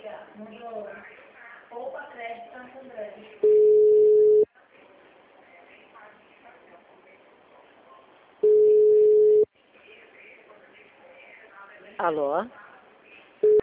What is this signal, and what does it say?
Ringback tone (Europe, 425 Hz), with speech.